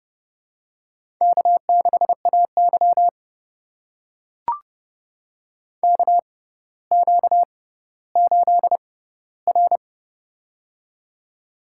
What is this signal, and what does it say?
Morse code, heard as a keyed tone, about 30 wpm, spaced as for 15 wpm, 700 Hz. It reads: K6AY KQ8R
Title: USA Call Signs - 4 Characters - 6x Character Spacing - 30wpm